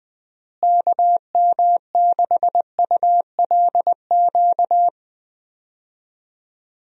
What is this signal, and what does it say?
Morse code, heard as a keyed tone, about 21 wpm, 700 Hz. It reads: KM6ULQ
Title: USA Call Signs - 6 Characters - Repeated 3x - 20wpm